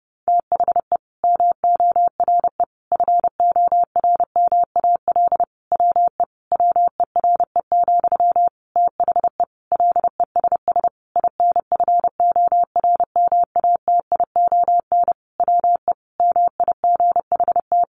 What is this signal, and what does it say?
Morse code, heard as a keyed tone, about 30 wpm, 700 Hz. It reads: THE MORE FORMAL WE WERE, THE LESS INFORMATION WE MIGHT
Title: Sentences from Top 1300 Words 30wpm